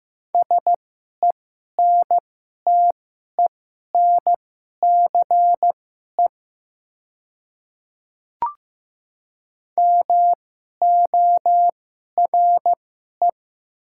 Morse code, heard as a keyed tone, about 15 wpm, spaced as for 11 wpm, 700 Hz. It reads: SENTENCE MORE
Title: 2x Character Spacing - Top 200 Words - 15wpm